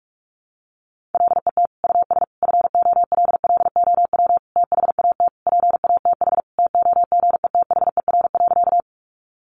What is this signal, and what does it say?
Morse code, heard as a keyed tone, about 44 wpm, 700 Hz. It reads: LET US FOLLOW THAT PATH TOGETHER.
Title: Sentences from Top 1400 Words 45wpm